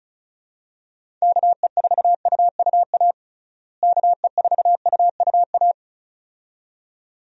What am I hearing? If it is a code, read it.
Morse code, heard as a keyed tone, about 36 wpm, 700 Hz. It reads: KE4UUA KE4UUA
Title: USA Call Signs - 6 Characters - Repeated 2x - 35wpm